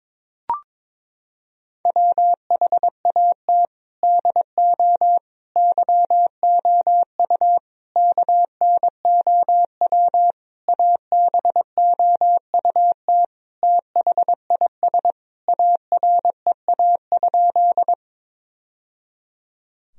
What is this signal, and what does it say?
Morse code, heard as a keyed tone, about 22 wpm, 700 Hz. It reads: WHAT DO YOU KNOW ABOUT THIS AREA?
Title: Sentences from Top 1000 Words - Review Part 2 - 22wpm